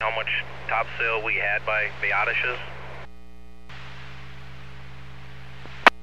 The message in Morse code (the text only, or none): none